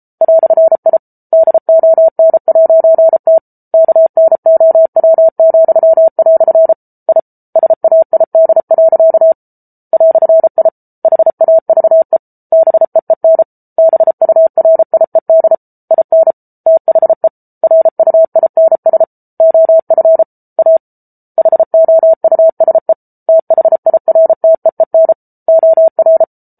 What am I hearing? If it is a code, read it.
Morse code, heard as a keyed tone, about 34 wpm, 650 Hz. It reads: "I DON'T KNOW," I SAID. "I HAVE BEEN BURIED IN THE RUINS OF A HOUSE THIRTEEN OR